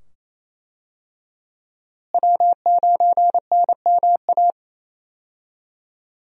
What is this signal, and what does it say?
Morse code, heard as a keyed tone, about 28 wpm, 700 Hz. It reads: W9NMA
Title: USA Call Signs - 5 Characters - Repeated 2x - 28wpm